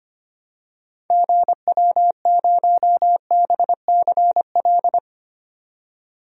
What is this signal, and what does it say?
Morse code, heard as a keyed tone, about 25 wpm, 700 Hz. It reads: GW0BCL
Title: DXCC-100 Call Signs - 6 Characters - 25wpm